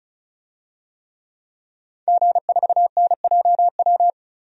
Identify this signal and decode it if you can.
Morse code, heard as a keyed tone, about 36 wpm, 700 Hz. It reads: G4NJW
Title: DXCC-100 Call Signs - 5 Characters - 35wpm